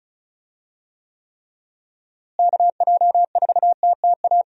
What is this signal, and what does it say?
Morse code, heard as a keyed tone, about 36 wpm, 700 Hz. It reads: KJ4TTA
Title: USA Call Signs - 6 Characters - Repeated 3x - 35wpm